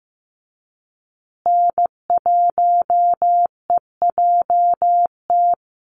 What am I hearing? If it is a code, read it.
Morse code, heard as a keyed tone, about 14 wpm, 700 Hz. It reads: N1EJT
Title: USA Call Signs Repeated 2x 15wpm